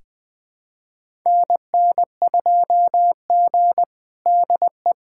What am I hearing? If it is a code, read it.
Morse code, heard as a keyed tone, about 20 wpm, 700 Hz. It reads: NN2G DE
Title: USA Callsign DE Callsign - 4 Characters - 20wpm